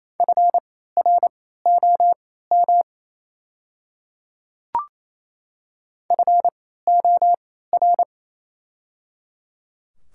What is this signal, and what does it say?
Morse code, heard as a keyed tone, about 27 wpm, spaced as for 16 wpm, 690 Hz. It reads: FROM FOR
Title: 3x Character Spacing - Top 200 Words - 28wpm